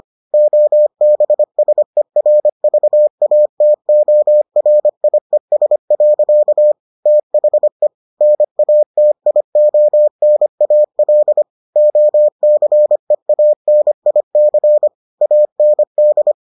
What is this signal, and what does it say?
Morse code, heard as a keyed tone, about 25 wpm, 600 Hz. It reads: OBSERVATORIES. THE NATIONAL OCEANIC AND